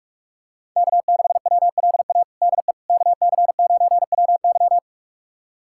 Morse code, heard as a keyed tone, about 46 wpm, 700 Hz. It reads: K6WLA DE KC9WY
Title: USA Callsign DE Callsign 45wpm